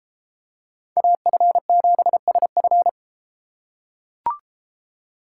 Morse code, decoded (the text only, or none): AF7SF